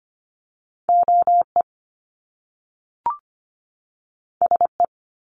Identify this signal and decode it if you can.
Morse code, heard as a keyed tone, about 25 wpm, 700 Hz. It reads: OE SE
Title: Top Two Letter Combinations in Words 25wpm